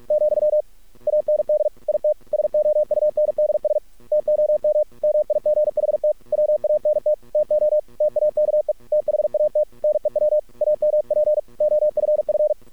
Morse code, heard as a keyed tone, about 48 wpm, 610 Hz. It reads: , AND IT SOUNDS TOM MIGHT WANT TO TAKE THAT NEW AMP OFF
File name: archived / CRSnet